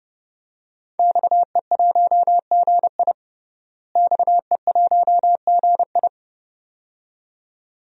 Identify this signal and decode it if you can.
Morse code, heard as a keyed tone, about 32 wpm, 700 Hz. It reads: XE1GI XE1GI
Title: DXCC-100 Call Signs - 5 Characters - Repeated 2x - 30wpm